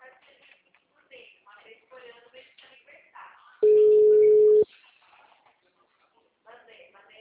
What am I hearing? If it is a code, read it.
Ringback tone (Europe, 425 Hz), with speech.
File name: out- -2019- - - .41622